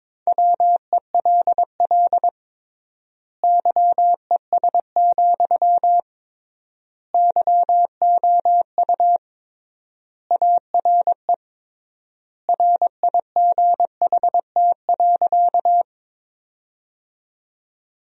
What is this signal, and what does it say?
Morse code, heard as a keyed tone, about 22 wpm, 700 Hz. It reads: WELL YES, YOU ARE RIGHT.
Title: 3x Word-Spacing - Sentences from Top 500 Words - 22wpm